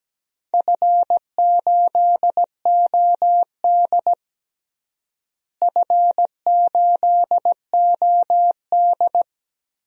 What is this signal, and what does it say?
Morse code, heard as a keyed tone, about 17 wpm, 700 Hz. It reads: F8OD F8OD
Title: DXCC-100 Call Signs - 4 Characters - Repeated 2x - 17wpm